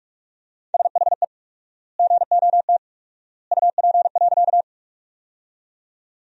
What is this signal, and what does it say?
Morse code, heard as a keyed tone, about 44 wpm, 700 Hz. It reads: SHE GOT UP.
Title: 4x Word-Spacing - Sentences from Top 300 Words - 45wpm